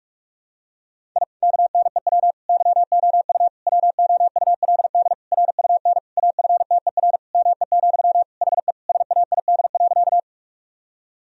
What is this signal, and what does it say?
Morse code, heard as a keyed tone, about 46 wpm, 700 Hz. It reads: I KNEW YOU WOULD RUN AFTER ME, HE SAID.
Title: Sentences from Top 1000 Words - Review Part 1 - 45wpm